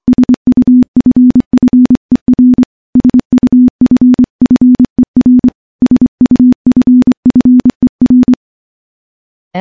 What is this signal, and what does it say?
Morse code, heard as a keyed tone, about 25 wpm, 250 Hz. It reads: SUFFER SUFFER SUFFER
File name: morse-training-25WPM